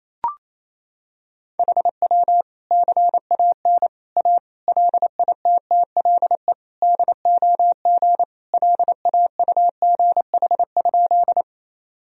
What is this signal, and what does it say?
Morse code, heard as a keyed tone, about 28 wpm, 700 Hz. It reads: HW CAN A LITTLE DOG LAUGH?
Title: Sentences from Top 500 Words and Abbreviations 28wpm